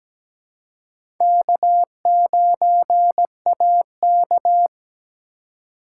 Morse code, heard as a keyed tone, about 17 wpm, 700 Hz. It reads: K9AK
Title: USA Call Signs - 2x Character Spacing - 17wpm